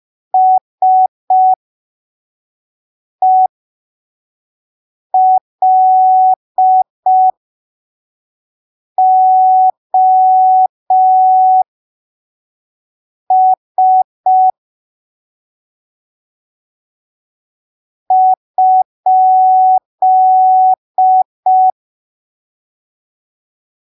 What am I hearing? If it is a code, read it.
Morse code, heard as a keyed tone, about 5 wpm, spaced as for 3 wpm, 750 Hz. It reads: SELOS?
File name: arquivos / texto / texto 16 05ppm